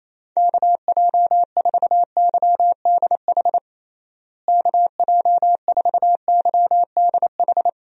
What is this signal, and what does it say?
Morse code, heard as a keyed tone, about 27 wpm, 700 Hz. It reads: KJ4YDH KJ4YDH
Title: USA Call Signs - 6 Characters - Repeated 2x - 28wpm